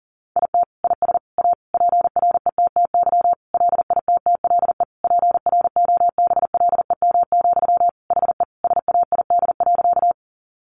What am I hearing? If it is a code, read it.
Morse code, heard as a keyed tone, about 40 wpm, 700 Hz. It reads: IT IS A PRETTY LITTLE PROBLEM, HE SAID.
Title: Sentences from Top 1000 Words 40wpm